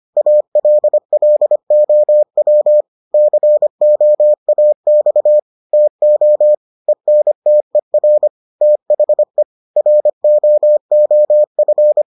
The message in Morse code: ALLOW COAX TO ENTER THE ROOF

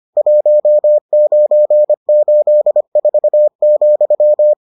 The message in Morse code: 1984,

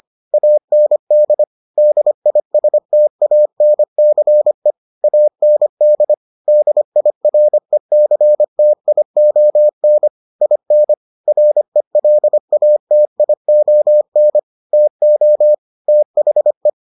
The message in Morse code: AND DISTANCE AND DIRECTION IN RELATION TO THE

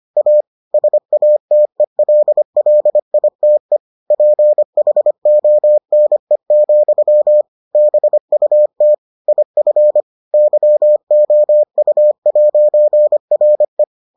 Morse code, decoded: A SATELLITE PHONE, BUT IF YOU'RE

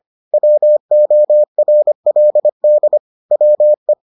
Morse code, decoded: WORLD WE

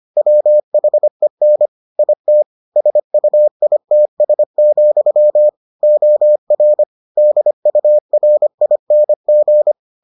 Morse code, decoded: WHEN IT SUITS, OR DURING

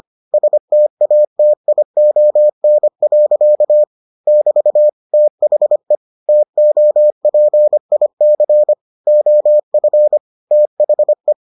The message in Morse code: STATION. <BT> THE TOPIC OF THE

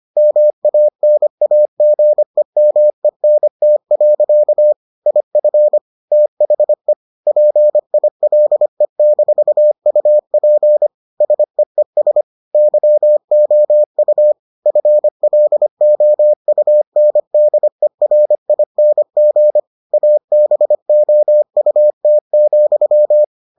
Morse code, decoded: MANAGEMENT. IF THE PILE-UP SEES YOU FLOUNDERING ABOUT,

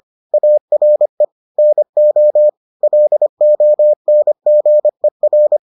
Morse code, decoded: ARE NO LONGER

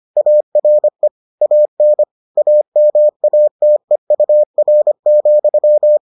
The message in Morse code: ARE AN AMATEUR,

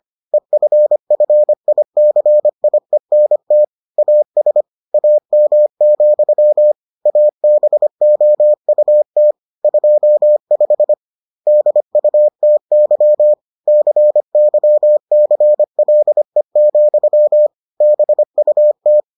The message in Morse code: EFFICIENT AS AM, ABOUT 25 DUTY CYCLE, BUT

